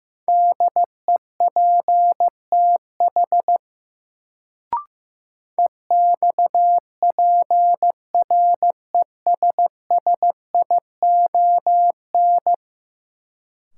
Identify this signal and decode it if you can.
Morse code, heard as a keyed tone, about 15 wpm, 700 Hz. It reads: DEPTH EXPRESSION
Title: Sentences from Top 1200 Words - New Word Drill Alt - 15wpm